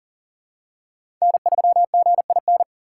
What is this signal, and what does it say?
Morse code, heard as a keyed tone, about 40 wpm, 700 Hz. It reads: N3GIN
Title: USA Call Signs - 5 Characters - 2x Character Spacing - 40wpm